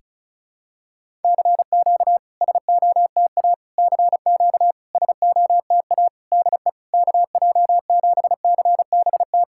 Morse code, heard as a keyed tone, about 36 wpm, 690 Hz. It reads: CQ SOTA CQ SOTA DE KJ7CBT
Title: SOTA QSO Practice 35wpm